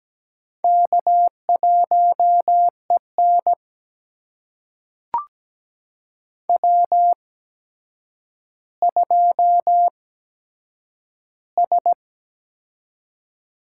Morse code, heard as a keyed tone, about 17 wpm, spaced as for 8 wpm, 700 Hz. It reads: K1EN W2S
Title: USA Call Signs - 4 Characters - 8x Character Spacing - 17wpm